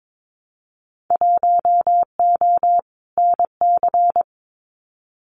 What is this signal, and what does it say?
Morse code, heard as a keyed tone, about 22 wpm, 700 Hz. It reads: 1O NC
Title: Winter Field Day Exchanges 22wpm